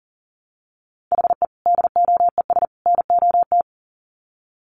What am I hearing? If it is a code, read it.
Morse code, heard as a keyed tone, about 40 wpm, 700 Hz. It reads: HE DOES NOT
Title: Sets of 3 Words 40wpm